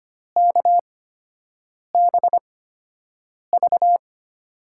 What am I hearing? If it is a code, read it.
Morse code, heard as a keyed tone, about 25 wpm, spaced as for 7 wpm, 700 Hz. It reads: KBV
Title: USA Call Signs - 6 Characters - 8x Character Spacing - 25wpm